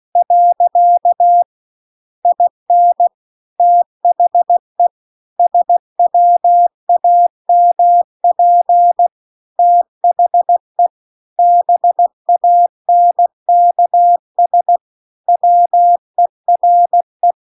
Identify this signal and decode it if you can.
Morse code, heard as a keyed tone, about 16 wpm, 700 Hz. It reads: . IN THE SWAMP THE BANKS WERE
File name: CW-The-Big-Twoheaded-River-2-16WPM